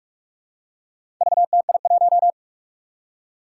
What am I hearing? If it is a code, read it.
Morse code, heard as a keyed tone, about 44 wpm, 700 Hz. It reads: VTI1
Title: Four Challenging Letter-Number Combinations 45wpm